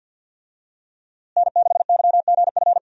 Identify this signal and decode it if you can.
Morse code, heard as a keyed tone, about 47 wpm, 700 Hz. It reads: N6YGF
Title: USA Call Signs - 5 Characters - Repeated 3x - 50wpm